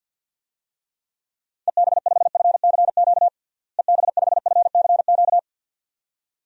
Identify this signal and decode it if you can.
Morse code, heard as a keyed tone, about 51 wpm, 700 Hz. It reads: EB5FCQ EB5FCQ
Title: DXCC-100 Call Signs - 6 Characters - Repeated 2x - 50wpm